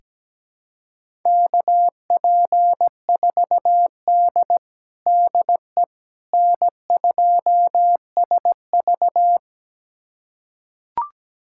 Morse code, heard as a keyed tone, about 17 wpm, 700 Hz. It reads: KP4D DE N2SV
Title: USA Callsign DE Callsign - 4 Characters - 17wpm